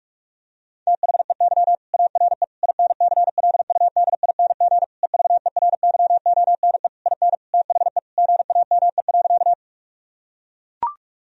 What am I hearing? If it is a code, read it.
Morse code, heard as a keyed tone, about 43 wpm, 700 Hz. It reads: THEY ARE INCLUDING EVERYONE IN THE GAME.
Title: Sentences from Top 1000 Words - Review Part 4 - 45wpm